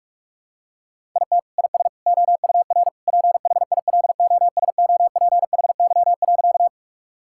Morse code, heard as a keyed tone, about 43 wpm, 700 Hz. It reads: IT IS OUR PHILOSOPHY.
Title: Sentences from Top 1000 Words - Review Part 6 - 45wpm